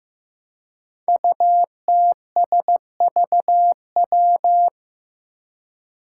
Morse code, heard as a keyed tone, about 15 wpm, 700 Hz. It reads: UTSVW
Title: Five Challenging Letter-Number Combinations 15wpm